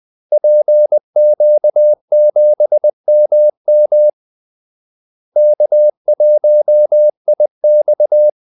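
Morse code, heard as a keyed tone, about 20 wpm, 600 Hz. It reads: PQ7MM K1IX
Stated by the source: Morse call-sign practice at 20 wpm.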